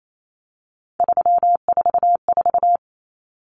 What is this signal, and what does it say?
Morse code, heard as a keyed tone, about 28 wpm, 700 Hz. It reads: 344
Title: Numbers - 3-Digit 28wpm